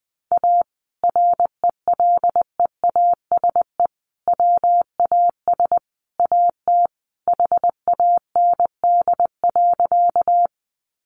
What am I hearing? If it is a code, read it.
Morse code, heard as a keyed tone, about 19 wpm, 700 Hz. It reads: A RELEASE WAS AT HAND.
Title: Sentences from Top 1100 Words 20wpm